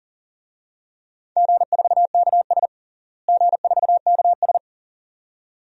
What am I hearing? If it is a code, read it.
Morse code, heard as a keyed tone, about 37 wpm, 700 Hz. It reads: G4KS G4KS
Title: DXCC-100 Call Signs - 4 Characters - Repeated 2x - 40wpm